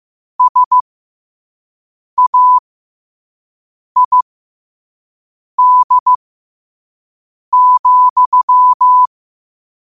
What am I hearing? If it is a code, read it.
Morse code, heard as a keyed tone, about 15 wpm, spaced as for 5 wpm, 1000 Hz. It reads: SAID,
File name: qotdfiles / 5wpm